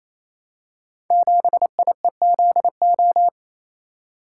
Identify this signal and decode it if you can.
Morse code, heard as a keyed tone, about 29 wpm, 700 Hz. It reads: 7IEZO
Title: Code Groups - 5 Letter:Numbers - 28wpm